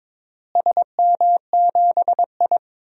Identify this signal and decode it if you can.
Morse code, heard as a keyed tone, about 22 wpm, 700 Hz. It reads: SM7I
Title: DXCC-100 Call Signs - 4 Characters - Repeated 2x - 22wpm